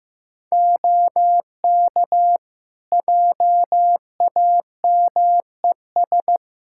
Morse code, heard as a keyed tone, about 15 wpm, 700 Hz. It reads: OK JAMES
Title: Practice QSO - Exchange 3 - More Info - 15wpm